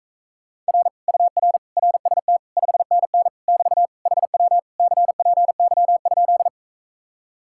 Morse code, heard as a keyed tone, about 44 wpm, 700 Hz. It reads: R UR RST 5NN <BT> HW CPY?